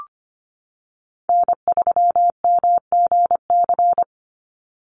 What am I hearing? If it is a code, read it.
Morse code, heard as a keyed tone, about 25 wpm, 690 Hz. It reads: N3MGC
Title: USA Call Signs 25wpm